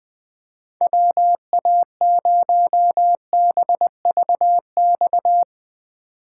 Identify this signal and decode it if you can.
Morse code, heard as a keyed tone, about 20 wpm, 700 Hz. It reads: WA0BVX